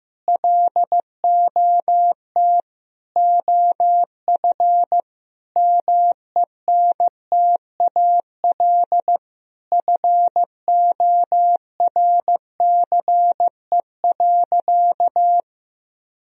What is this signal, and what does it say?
Morse code, heard as a keyed tone, about 16 wpm, 700 Hz. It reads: LOT OF MENTAL FORCE.
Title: Sentences from Top 1000 Words - Review Part 4 - 15wpm